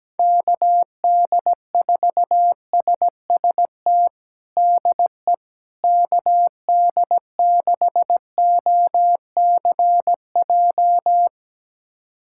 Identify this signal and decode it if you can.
Morse code, heard as a keyed tone, about 17 wpm, 700 Hz. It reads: KD4SST DE KD6OCJ
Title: USA Callsign DE Callsign - 6 Characters - 17wpm